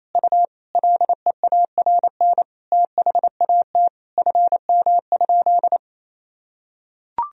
Morse code, heard as a keyed tone, about 28 wpm, 700 Hz. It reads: U LEARN THAT FM?
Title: Sentences from Top 300 Words and Abbreviations 28wpm